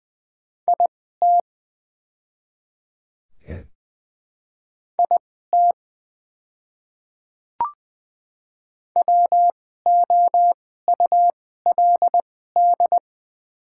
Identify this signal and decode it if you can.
Morse code, heard as a keyed tone, about 20 wpm, spaced as for 14 wpm, 700 Hz. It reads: IT IT WOULD
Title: 2x Character Spacing - Top 100 Words - 20wpm